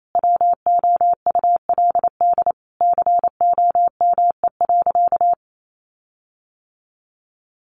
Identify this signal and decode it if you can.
Morse code, heard as a keyed tone, about 28 wpm, 700 Hz. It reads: WOULD COME.
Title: Sentences from Top 1500 Words 28wpm